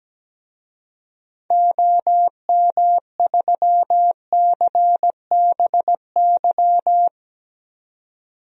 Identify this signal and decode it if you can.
Morse code, heard as a keyed tone, about 17 wpm, 700 Hz. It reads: OM3CBY
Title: DXCC-100 Call Signs - 6 Characters - 17wpm